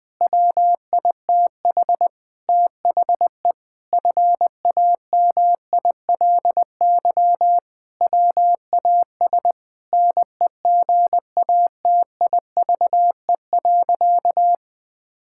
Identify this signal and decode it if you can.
Morse code, heard as a keyed tone, about 20 wpm, 700 Hz. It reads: WITH THE FAMILY WAS NEGATIVE.
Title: Sentences from Top 1000 Words - Review Part 6 - 20wpm